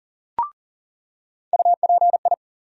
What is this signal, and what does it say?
Morse code, heard as a keyed tone, about 40 wpm, 700 Hz. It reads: UPI
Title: USA Call Sign Postfixes - 3 Letters 40wpm